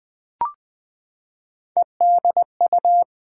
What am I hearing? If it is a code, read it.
Morse code, heard as a keyed tone, about 20 wpm, 700 Hz. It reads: EDU
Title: Build A Word - 3 Letters At A Time - 8-Letter Words - 20wpm